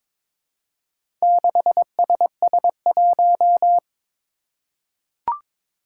Morse code, heard as a keyed tone, about 23 wpm, 700 Hz. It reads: 6SS1
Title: Four Challenging Letter-Number Combinations 22wpm